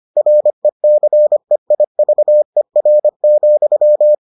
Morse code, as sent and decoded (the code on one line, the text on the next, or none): .-. -.-. .. ...- . .-. --..--
RECEIVER,